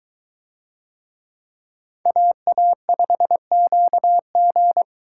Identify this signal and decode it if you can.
Morse code, heard as a keyed tone, about 21 wpm, 700 Hz. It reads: AA5QG